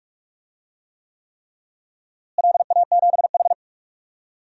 Morse code, heard as a keyed tone, about 46 wpm, 700 Hz. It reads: LA7H
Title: DXCC-100 Call Signs - 4 Characters - Repeated 3x - 45wpm